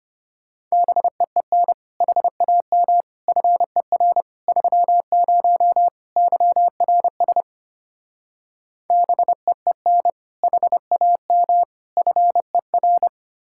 Morse code, heard as a keyed tone, about 32 wpm, 700 Hz. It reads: BEEN HAM FER 30 YRS BEEN HAM FER